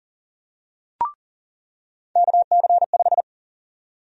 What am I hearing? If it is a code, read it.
Morse code, heard as a keyed tone, about 40 wpm, 700 Hz. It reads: KC5
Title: USA Call Sign Prefixes - 40wpm